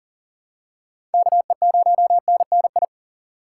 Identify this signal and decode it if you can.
Morse code, heard as a keyed tone, about 40 wpm, 700 Hz. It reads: KE0NNI
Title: USA Call Signs - 4x Character Spacing - 40wpm